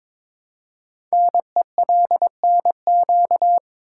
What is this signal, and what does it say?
Morse code, heard as a keyed tone, about 22 wpm, 700 Hz. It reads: NELNQ
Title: Code Groups - 5 Letters - 22wpm